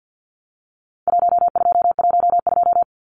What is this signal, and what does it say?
Morse code, heard as a keyed tone, about 50 wpm, 700 Hz. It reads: 1212
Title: Numbers - Contest Style - 4-Digit 50wpm